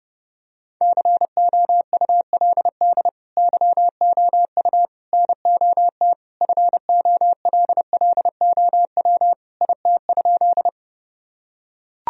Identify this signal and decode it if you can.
Morse code, heard as a keyed tone, about 30 wpm, 700 Hz. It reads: COULD YOU NOT FOLLOW IT?
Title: Sentences from Top 1000 Words - Review Part 1 - 30wpm